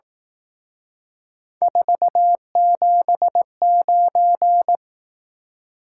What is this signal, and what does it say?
Morse code, heard as a keyed tone, about 18 wpm, 700 Hz. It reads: 479
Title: Speed-Racing Top 180 QSO Elements 15wpm